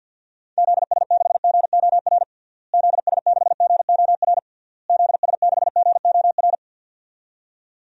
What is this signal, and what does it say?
Morse code, heard as a keyed tone, about 52 wpm, 700 Hz. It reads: ZS6GOR ZS6GOR ZS6GOR
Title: DXCC-100 Call Signs - 6 Characters - Repeated 3x - 50wpm